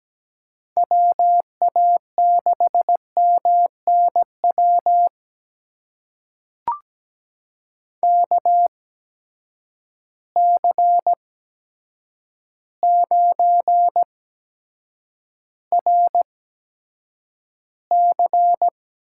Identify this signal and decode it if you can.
Morse code, heard as a keyed tone, about 17 wpm, spaced as for 8 wpm, 700 Hz. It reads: WA6MNW KC9RC
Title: USA Call Signs - 6 Characters - 8x Character Spacing - 17wpm